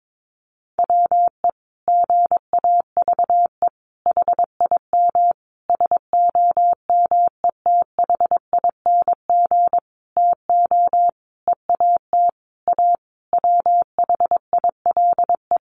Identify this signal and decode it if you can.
Morse code, heard as a keyed tone, about 22 wpm, 700 Hz. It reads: WE GAVE HIM SOMETHING TO EAT A WHILE
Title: Sentences from Top 600 Words 22wpm